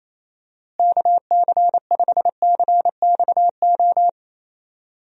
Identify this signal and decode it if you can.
Morse code, heard as a keyed tone, about 28 wpm, 700 Hz. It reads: KC5CXO
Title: USA Call Signs - 6 Characters - Repeated 2x - 28wpm